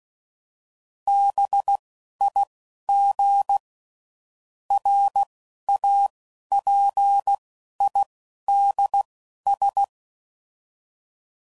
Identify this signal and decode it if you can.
Morse code, heard as a keyed tone, about 16 wpm, spaced as for 11 wpm, 800 Hz. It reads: BIG RAPIDS
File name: CW Code Groups @10WPM / fromSM's / Cities